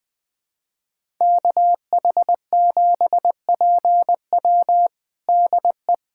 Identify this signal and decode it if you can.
Morse code, heard as a keyed tone, about 20 wpm, 700 Hz. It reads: KH7PW DE
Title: USA Callsign DE Callsign - 5 Characters - 20wpm